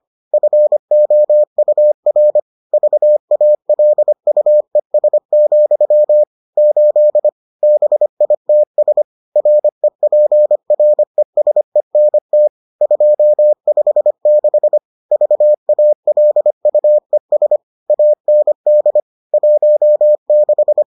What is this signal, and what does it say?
Morse code, heard as a keyed tone, about 25 wpm, 600 Hz. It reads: FOUR VALUES, 8 BITS REPRESENT 256 VALUES AND 16